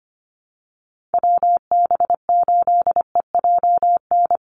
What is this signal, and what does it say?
Morse code, heard as a keyed tone, about 25 wpm, 700 Hz. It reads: WB8EJN
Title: Call Signs Repeated 3x - 25wpm